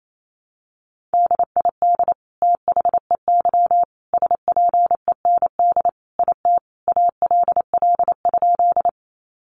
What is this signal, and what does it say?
Morse code, heard as a keyed tone, about 28 wpm, 700 Hz. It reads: DID THEY SPEND IT ALL?
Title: Sentences from Top 800 Words 28wpm